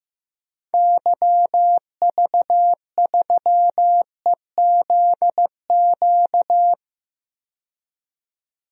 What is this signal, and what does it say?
Morse code, heard as a keyed tone, about 15 wpm, 700 Hz. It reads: YV3EZQ
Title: DXCC-100 Call Signs - 6 Characters - 15wpm